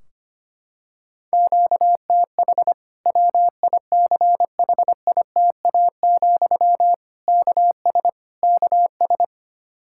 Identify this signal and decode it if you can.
Morse code, heard as a keyed tone, about 25 wpm, 700 Hz. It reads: QTH WICHITA, KS KS
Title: Practice QSO - Exchange 1 - Beginning - 25wpm